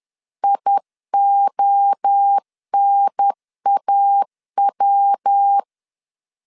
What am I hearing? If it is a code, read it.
Morse code, heard as a keyed tone, about 10 wpm, 800 Hz. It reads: IONAW
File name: CW / DET / 1 3 1500C S C 10wpm 3335